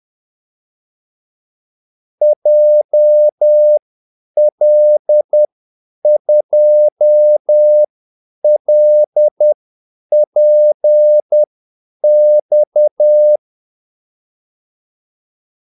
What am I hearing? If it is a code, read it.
Morse code, heard as a keyed tone, about 10 wpm, spaced as for 8 wpm, 600 Hz. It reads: JL2LPX
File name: Callsigns-10-8wpm